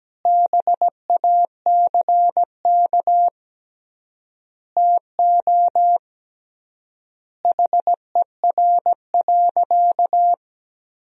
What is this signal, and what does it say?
Morse code, heard as a keyed tone, about 17 wpm, 700 Hz. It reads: BACK TO HER.
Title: 3x Word-Spacing - Sentences from Top 300 Words - 17wpm